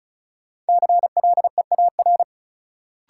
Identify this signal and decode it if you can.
Morse code, heard as a keyed tone, about 36 wpm, 700 Hz. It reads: CLEAR
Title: Sets of 5 Words Encore - New Word Drill - 35wpm